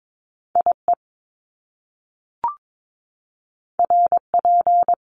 Morse code, heard as a keyed tone, about 22 wpm, 700 Hz. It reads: IE RP
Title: Top Two Letter Combinations in Words 22wpm